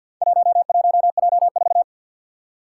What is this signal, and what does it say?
Morse code, heard as a keyed tone, about 50 wpm, 700 Hz. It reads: 11J4
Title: Four Challenging Letter-Number Combinations 50wpm